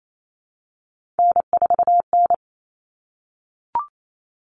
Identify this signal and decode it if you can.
Morse code, heard as a keyed tone, about 28 wpm, 700 Hz. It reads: N4N
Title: Common Cut Numbers - 3-Digit 28wpm